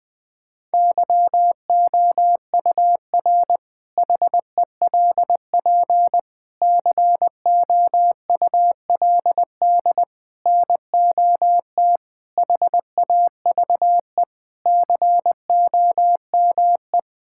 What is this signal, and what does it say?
Morse code, heard as a keyed tone, about 20 wpm, 700 Hz. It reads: YOUR HELP COULD NOT HAVE COME
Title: Sentences with 8-Word Phrases - Spoken Beginning - 20wpm